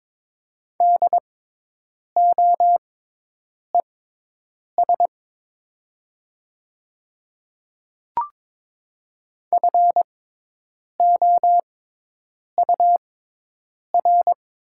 Morse code, heard as a keyed tone, about 22 wpm, spaced as for 8 wpm, 700 Hz. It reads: DOES FOUR
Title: 6x Character Spacing - Top 300 Words - 22wpm